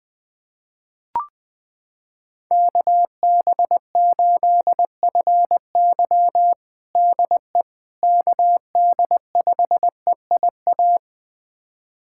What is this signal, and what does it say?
Morse code, heard as a keyed tone, about 21 wpm, 700 Hz. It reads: KB8FY DE KD5EIA
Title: USA Callsign DE Callsign 20wpm